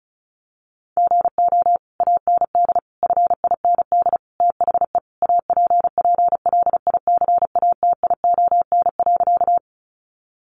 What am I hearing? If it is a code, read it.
Morse code, heard as a keyed tone, about 34 wpm, 700 Hz. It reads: GO AND FIND THE APPLICATION.
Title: Sentences from Top 1200 Words 35wpm